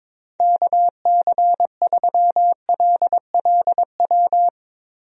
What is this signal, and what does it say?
Morse code, heard as a keyed tone, about 22 wpm, 700 Hz. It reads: KC3LLW
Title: USA Call Signs - 6 Characters - Repeated 3x - 22wpm